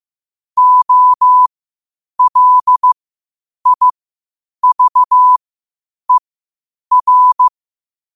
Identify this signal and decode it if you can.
Morse code, heard as a keyed tone, about 15 wpm, spaced as for 8 wpm, 1000 Hz. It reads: OLIVER
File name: qotdfiles / 8wpm